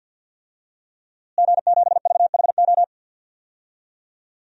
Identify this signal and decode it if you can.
Morse code, heard as a keyed tone, about 48 wpm, 700 Hz. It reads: G7VHO
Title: DXCC-100 Call Signs - 5 Characters - Repeated 2x - 50wpm